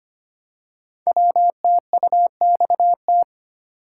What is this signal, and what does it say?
Morse code, heard as a keyed tone, about 25 wpm, 700 Hz. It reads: WTUXT
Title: Code Groups - 5 Letters - 25wpm